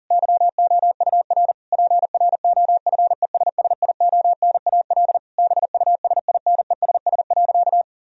Morse code, heard as a keyed tone, about 40 wpm, 700 Hz. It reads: YOUR PROFESSIONAL BUSINESS.
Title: Sentences from Top 1000 Words - Review Part 4 - 40wpm